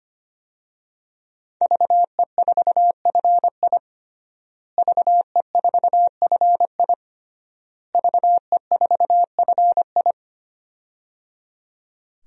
Morse code, heard as a keyed tone, about 25 wpm, 700 Hz. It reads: VE4FI VE4FI VE4FI
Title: DXCC-100 Call Signs - 5 Characters - Repeated 3x - 25wpm